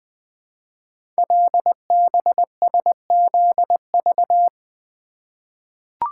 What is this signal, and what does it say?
Morse code, heard as a keyed tone, about 20 wpm, 700 Hz. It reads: LBSZV
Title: Code Groups - 5 Letters - 20wpm